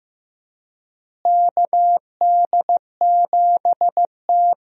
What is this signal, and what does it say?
Morse code, heard as a keyed tone, about 15 wpm, 700 Hz. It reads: KD7T
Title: USA Call Signs - 4 Characters - Repeated 2x - 15wpm